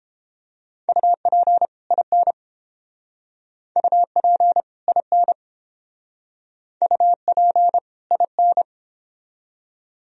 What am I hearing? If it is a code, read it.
Morse code, heard as a keyed tone, about 31 wpm, 700 Hz. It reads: UP IN UP IN UP IN